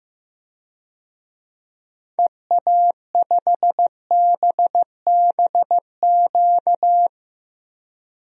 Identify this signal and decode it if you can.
Morse code, heard as a keyed tone, about 16 wpm, 700 Hz. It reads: EA5BBQ
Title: DXCC-100 Call Signs - 6 Characters - Repeated 2x - 15wpm